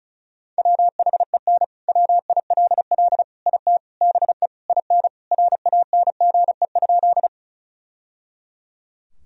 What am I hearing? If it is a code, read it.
Morse code, heard as a keyed tone, about 36 wpm, 700 Hz. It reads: WHEN WILL IT BE IN RANGE?
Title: Sentences from Top 1000 Words - Review Part 4 - 35wpm